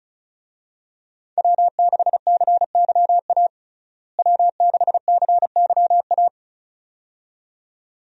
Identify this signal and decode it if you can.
Morse code, heard as a keyed tone, about 34 wpm, 700 Hz. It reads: W6CYA W6CYA
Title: USA Call Signs - 5 Characters - Repeated 2x - 35wpm